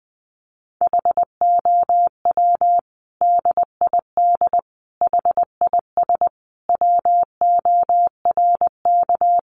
Morse code, heard as a keyed tone, about 20 wpm, 700 Hz. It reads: HOW DID HIS WORK
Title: Sentences from Top 900 Words 20wpm